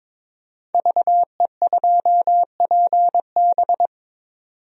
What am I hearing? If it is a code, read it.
Morse code, heard as a keyed tone, about 22 wpm, 700 Hz. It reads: VE2PB
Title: DXCC-100 Call Signs - 5 Characters - 22wpm